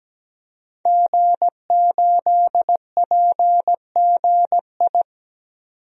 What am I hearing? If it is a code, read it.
Morse code, heard as a keyed tone, about 17 wpm, 700 Hz. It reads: G8PGI
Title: DXCC-100 Call Signs - 5 Characters - 17wpm